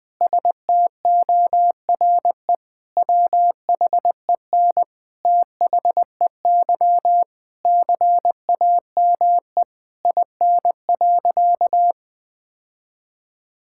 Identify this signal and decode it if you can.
Morse code, heard as a keyed tone, about 20 wpm, 700 Hz. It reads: STORE WHEN THEY CAME IN.
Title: Sentences from Top 1000 Words - Review Part 5 - 20wpm